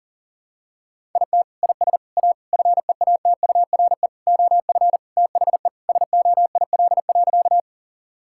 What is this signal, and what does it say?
Morse code, heard as a keyed tone, about 40 wpm, 700 Hz. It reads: IT IS A FEATURE OF THE SOIL.
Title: Sentences from Top 1000 Words - Review Part 5 - 40wpm